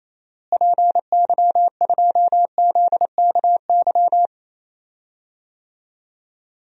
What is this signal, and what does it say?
Morse code, heard as a keyed tone, about 28 wpm, 700 Hz. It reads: PY2ZKY
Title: DXCC-100 Call Signs - 6 Characters - Repeated 3x - 28wpm